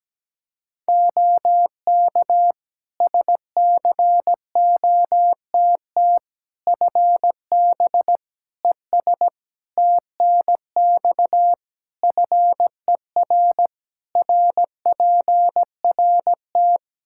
Morse code, heard as a keyed tone, about 17 wpm, 700 Hz. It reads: OK SCOTT FB ES TNX FER RPRT
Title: Practice QSO - Exchange 2 - Info - 17wpm